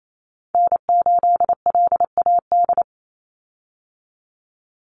Morse code, heard as a keyed tone, about 28 wpm, 700 Hz. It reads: N8LAD
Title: USA Call Signs 28wpm